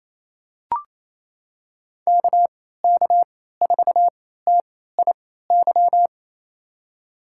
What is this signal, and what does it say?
Morse code, heard as a keyed tone, about 28 wpm, spaced as for 16 wpm, 700 Hz. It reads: KK4TIY
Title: USA Call Signs - 6 Characters - 3x Character Spacing - 28wpm